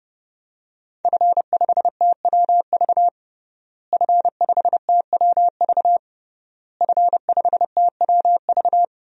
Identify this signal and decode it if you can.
Morse code, heard as a keyed tone, about 30 wpm, 700 Hz. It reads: F5TWV F5TWV F5TWV
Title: DXCC-100 Call Signs - 5 Characters - Repeated 3x - 30wpm